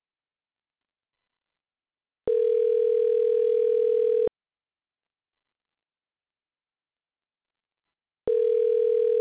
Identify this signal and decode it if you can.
Ringback tone (North America), 440 + 480 Hz.